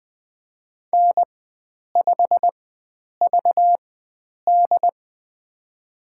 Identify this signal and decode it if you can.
Morse code, heard as a keyed tone, about 21 wpm, spaced as for 9 wpm, 700 Hz. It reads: N5VD
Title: USA Call Signs - 4 Characters - 4x Character Spacing - 20wpm